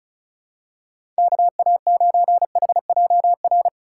Morse code, heard as a keyed tone, about 34 wpm, 700 Hz. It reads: KA9HJR
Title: USA Call Signs - 6 Characters - 6x Character Spacing - 35wpm